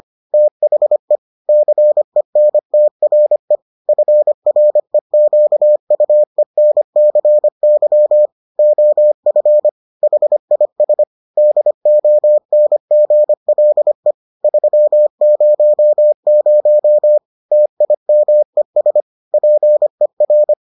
Morse code, heard as a keyed tone, about 25 wpm, 600 Hz. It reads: THE CENTRE FREQUENCY OF HIS DONGLE 300 TIMES PER